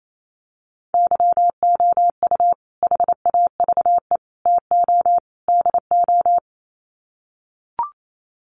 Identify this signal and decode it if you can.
Morse code, heard as a keyed tone, about 28 wpm, 700 Hz. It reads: YOU HAVE TO DO
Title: Sets of 4 Words 28wpm